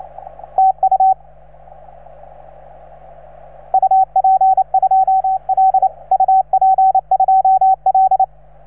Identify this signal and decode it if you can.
Morse code, heard as a keyed tone, about 28 wpm, 750 Hz, with radio noise. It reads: TU UP2L UP2L